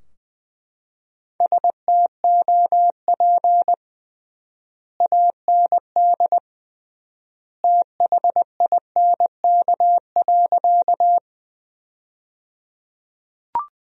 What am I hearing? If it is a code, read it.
Morse code, heard as a keyed tone, about 20 wpm, 700 Hz. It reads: STOP AND THINK.
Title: 3x Word-Spacing - Sentences from Top 300 Words - 20wpm